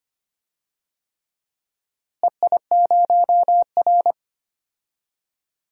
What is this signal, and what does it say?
Morse code, heard as a keyed tone, about 25 wpm, 700 Hz. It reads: EI0R
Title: DXCC-100 Call Signs - 4 Characters - Repeated 3x - 25wpm